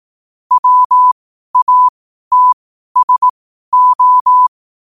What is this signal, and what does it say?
Morse code, heard as a keyed tone, about 18 wpm, spaced as for 12 wpm, 1000 Hz. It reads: WATSO